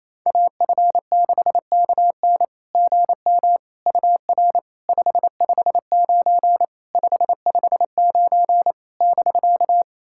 Morse code, heard as a keyed tone, about 28 wpm, 690 Hz. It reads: AF6KN GM UR 559 559 <BK>
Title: SOTA QSO Practice 28wpm